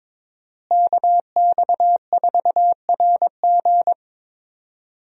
Morse code, heard as a keyed tone, about 22 wpm, 700 Hz. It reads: KX4RG
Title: USA Call Signs - 5 Characters - Repeated 2x - 22wpm